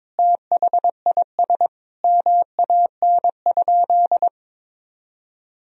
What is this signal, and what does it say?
Morse code, heard as a keyed tone, about 21 wpm, 700 Hz. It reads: THIS MAN?
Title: Sentences from Top 200 Words and Abbreviations 22wpm